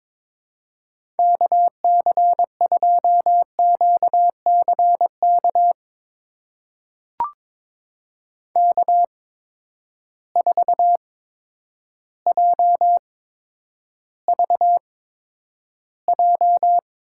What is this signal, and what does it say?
Morse code, heard as a keyed tone, about 22 wpm, spaced as for 10 wpm, 700 Hz. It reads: KC2QCK K4JVJ
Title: USA Call Signs - 8x Character Spacing - 22wpm